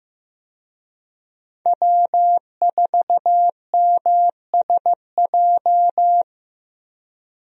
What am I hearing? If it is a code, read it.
Morse code, heard as a keyed tone, about 15 wpm, 700 Hz. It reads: W4MSJ
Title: USA Call Signs - 5 Characters - Repeated 2x - 15wpm